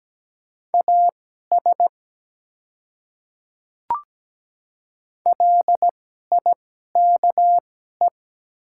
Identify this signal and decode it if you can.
Morse code, heard as a keyed tone, about 17 wpm, spaced as for 12 wpm, 700 Hz. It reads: AS LIKE
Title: 2x Character Spacing - Top 100 Words - 17wpm